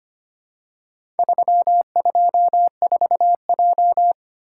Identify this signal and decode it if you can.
Morse code, heard as a keyed tone, about 25 wpm, 700 Hz. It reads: 324J
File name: practice / 25wpm / Four Letter-Number Combinations - 25wpm